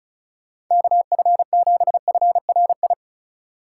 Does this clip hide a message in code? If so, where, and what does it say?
Yes: Morse code from 0.7 to 2.9 s, heard as a keyed tone, about 45 wpm, 700 Hz. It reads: KF7FRI